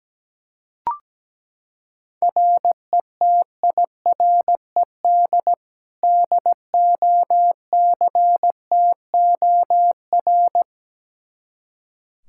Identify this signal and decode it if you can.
Morse code, heard as a keyed tone, about 17 wpm, 700 Hz. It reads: RETIRED DOCTOR
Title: Practice QSO - Exchange 3 - More Info - 17wpm